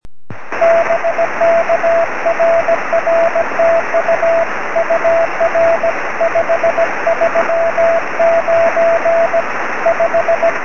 Morse code, heard as a keyed tone, about 17 wpm, 670 Hz, with radio noise. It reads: BK RR TU UR 539 5
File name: sm7lcb / 20-oz1ctz 23cm